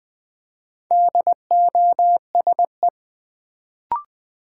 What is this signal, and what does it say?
Morse code, heard as a keyed tone, about 20 wpm, 690 Hz. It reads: DOSE
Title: Build A Word - 2 Letters At A Time - 4-Letter Words - 20wpm